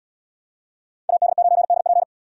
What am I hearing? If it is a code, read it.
Morse code, heard as a keyed tone, about 68 wpm, 700 Hz. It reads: AD9DP